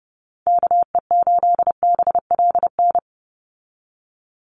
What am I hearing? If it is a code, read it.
Morse code, heard as a keyed tone, about 30 wpm, 700 Hz. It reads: KE8BLN
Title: USA Call Signs 30wpm